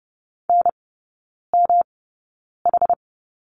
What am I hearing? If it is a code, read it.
Morse code, heard as a keyed tone, about 30 wpm, spaced as for 9 wpm, 700 Hz. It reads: NMH
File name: Lesson 18e - Letter m - Three Characters 30wpm